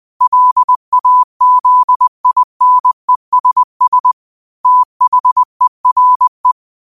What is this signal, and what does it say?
Morse code, heard as a keyed tone, about 20 wpm, 1000 Hz. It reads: LAZINESS THERE
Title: 20 WPM morse code quotes for Thu, 29 Feb 2024